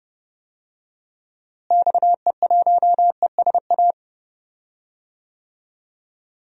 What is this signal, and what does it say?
Morse code, heard as a keyed tone, about 30 wpm, 700 Hz. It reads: XE1ESA
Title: DXCC-100 Call Signs - 6 Characters - Repeated 2x - 30wpm